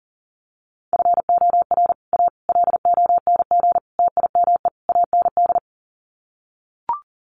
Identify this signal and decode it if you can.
Morse code, heard as a keyed tone, about 40 wpm, 700 Hz. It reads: FOR A LONG TIME AND
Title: Sets of 5 Words Encore2 40wpm